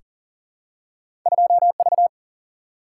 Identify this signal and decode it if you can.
Morse code, heard as a keyed tone, about 40 wpm, 700 Hz. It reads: 2V
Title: Single-Letter Single-Number 40wpm